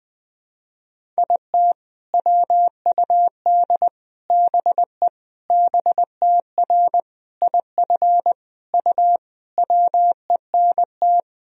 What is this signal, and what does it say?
Morse code, heard as a keyed tone, about 20 wpm, 700 Hz. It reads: IT WUD BE BTR IF U WENT
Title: Sentences from Top 400 Words and Abbreviations 20wpm